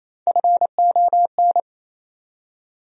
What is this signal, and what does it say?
Morse code, heard as a keyed tone, about 28 wpm, 700 Hz. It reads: FON
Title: USA Call Sign Postfixes - 3 Letters 28wpm